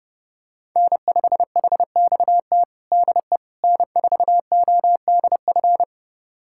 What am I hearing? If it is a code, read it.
Morse code, heard as a keyed tone, about 30 wpm, 700 Hz. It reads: N5HXT DE N4ODF
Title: USA Callsign DE Callsign - 5 Characters - 30wpm